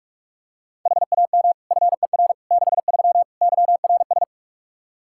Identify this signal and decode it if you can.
Morse code, heard as a keyed tone, about 47 wpm, 700 Hz. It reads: HAM FER 63 YRS
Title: Practice QSO - Exchange 3 - More Info - 45wpm